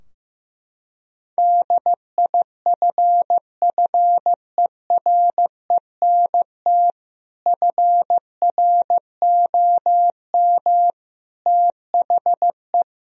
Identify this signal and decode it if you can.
Morse code, heard as a keyed tone, about 15 wpm, 700 Hz. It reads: DIFFERENT FROM THE
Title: Sentences from Top 1000 Words - Review Part 3 - 15wpm